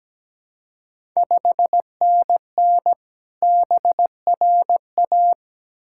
Morse code, heard as a keyed tone, about 17 wpm, 700 Hz. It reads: 5NN BRA
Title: Ontario QSO Party Exchange 17wpm